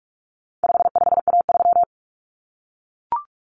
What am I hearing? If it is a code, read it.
Morse code, heard as a keyed tone, about 44 wpm, 700 Hz. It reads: 55A3
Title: Common Cut Numbers - 4-Digit 45wpm